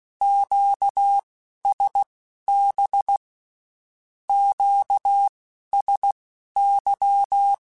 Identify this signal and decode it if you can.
Morse code, heard as a keyed tone, about 16 wpm, spaced as for 11 wpm, 800 Hz. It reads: QSB QSY
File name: (spaced) / Q-Calls Spaced-mp3s / Q Calls sp Once @15